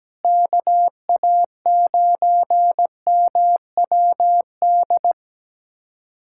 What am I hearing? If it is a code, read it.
Morse code, heard as a keyed tone, about 17 wpm, 700 Hz. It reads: KA9MWD
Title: USA Callsign DE Callsign - 6 Characters - 17wpm